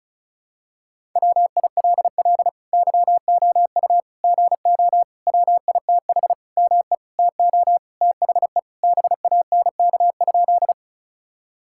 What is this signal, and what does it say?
Morse code, heard as a keyed tone, about 34 wpm, 700 Hz. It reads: WILL YOU GO WITH ME TO THE BANK?
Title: Sentences from Top 1000 Words - Review Part 6 - 35wpm